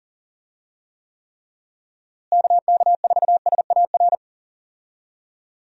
Morse code, heard as a keyed tone, about 40 wpm, 700 Hz. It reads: KK4SAR
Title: USA Call Signs - 6 Characters - Repeated 3x - 40wpm